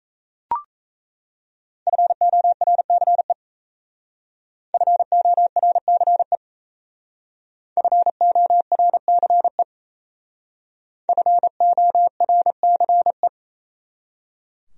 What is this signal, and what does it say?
Morse code, heard as a keyed tone, about 35 wpm, 700 Hz. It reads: FORCE FORCE FORCE FORCE